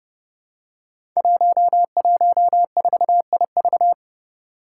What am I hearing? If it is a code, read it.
Morse code, heard as a keyed tone, about 30 wpm, 700 Hz. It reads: 114IV
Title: Five Challenging Letter-Number Combinations 30wpm